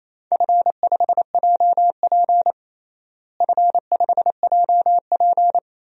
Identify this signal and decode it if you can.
Morse code, heard as a keyed tone, about 28 wpm, 700 Hz. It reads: F5JP F5JP
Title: DXCC-100 Call Signs - 4 Characters - Repeated 2x - 28wpm